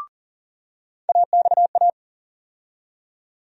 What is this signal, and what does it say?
Morse code, heard as a keyed tone, about 40 wpm, 700 Hz. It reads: AXA
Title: USA Call Sign Postfixes - 3 Letters 40wpm